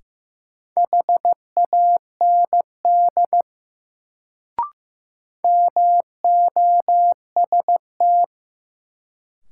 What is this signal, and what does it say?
Morse code, heard as a keyed tone, about 15 wpm, 700 Hz. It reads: HAND MOST
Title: Sentences from Top 200 Words - New Word Drill - 15wpm